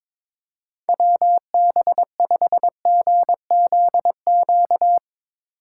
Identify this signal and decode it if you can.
Morse code, heard as a keyed tone, about 22 wpm, 700 Hz. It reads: WB5GZQ
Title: USA Call Signs - 6 Characters - 4x Character Spacing - 22wpm